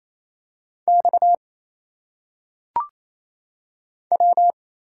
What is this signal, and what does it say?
Morse code, heard as a keyed tone, about 27 wpm, 700 Hz. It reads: X W
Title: Single Letters - Phonetic Pronunciation 28wpm